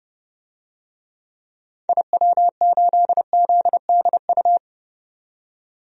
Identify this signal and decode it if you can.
Morse code, heard as a keyed tone, about 30 wpm, 690 Hz. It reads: IW8ZDU
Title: DXCC-100 Call Signs - 6 Characters - 30wpm